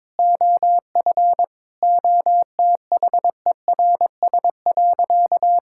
Morse code, heard as a keyed tone, about 22 wpm, 700 Hz. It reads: OF OTHERS.
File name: practice / 22wpm / Sentences with 4-Word Phrases - 22wpm